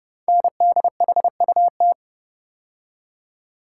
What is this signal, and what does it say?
Morse code, heard as a keyed tone, about 28 wpm, 700 Hz. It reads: NDHUT
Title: Code Groups - 5 Letters - 30wpm